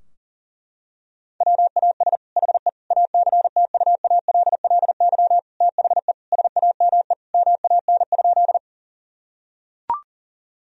Morse code, heard as a keyed tone, about 40 wpm, 700 Hz. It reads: WAS HE ACTUALLY THE SAME MAN?
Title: Sentences from Top 1000 Words - Review Part 4 - 40wpm